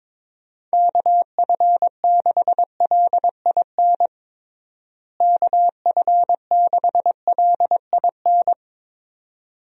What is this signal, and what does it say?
Morse code, heard as a keyed tone, about 22 wpm, 700 Hz. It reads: KF6LIN KF6LIN
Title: USA Call Signs - 6 Characters - Repeated 2x - 22wpm